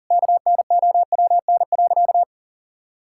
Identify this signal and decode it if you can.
Morse code, heard as a keyed tone, about 37 wpm, 700 Hz. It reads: KNOWN.
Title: Sentences from Top 1600 Words 40wpm